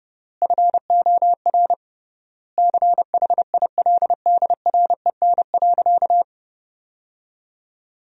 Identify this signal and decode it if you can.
Morse code, heard as a keyed tone, about 30 wpm, 700 Hz. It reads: FOR CHILDREN.
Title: 3x Word-Spacing - Sentences from Top 400 Words - 30wpm